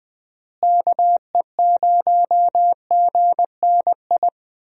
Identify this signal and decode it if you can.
Morse code, heard as a keyed tone, about 20 wpm, 700 Hz. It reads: KE0GNI
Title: USA Call Signs - 6 Characters - Repeated 2x - 20wpm